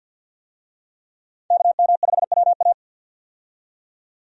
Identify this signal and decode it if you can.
Morse code, heard as a keyed tone, about 48 wpm, 700 Hz. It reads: KM5WA
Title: USA Call Signs - 5 Characters - 50wpm